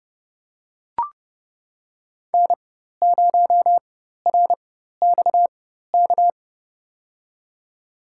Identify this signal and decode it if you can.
Morse code, heard as a keyed tone, about 30 wpm, spaced as for 14 wpm, 700 Hz. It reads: N0RXK
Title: USA Call Signs - 5 Characters - 4x Character Spacing - 30wpm